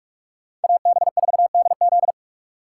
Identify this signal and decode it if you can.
Morse code, heard as a keyed tone, about 46 wpm, 700 Hz. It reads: AB4DZ
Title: USA Call Signs - 5 Characters - 3x Character Spacing - 45wpm